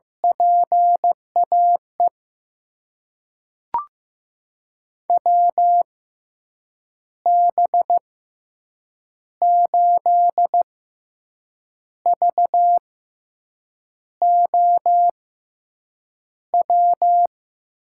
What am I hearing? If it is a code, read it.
Morse code, heard as a keyed tone, about 15 wpm, spaced as for 6 wpm, 700 Hz. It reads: PAE WB8VOW
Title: USA Call Signs - 6 Characters - 6x Character Spacing - 15wpm